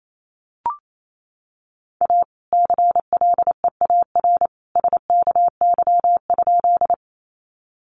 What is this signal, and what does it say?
Morse code, heard as a keyed tone, about 28 wpm, 700 Hz. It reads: A CLEAR SKY?
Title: Sentences from Top 1200 Words 28wpm